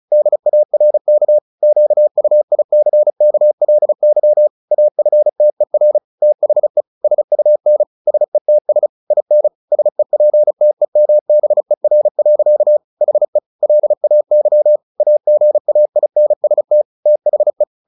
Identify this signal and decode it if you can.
Morse code, heard as a keyed tone, about 34 wpm, 600 Hz. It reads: DARK QUICKLY AFTER THE SUN SETS IN SEPTEMBER. HE LAY AGAINST THE
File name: the old man and the sea 2 35wpm